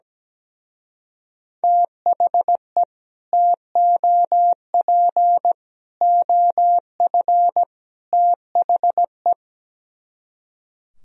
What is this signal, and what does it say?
Morse code, heard as a keyed tone, about 17 wpm, 700 Hz. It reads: THE TOP OF THE
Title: Speed-Racing - Sets of 4 Words 17wpm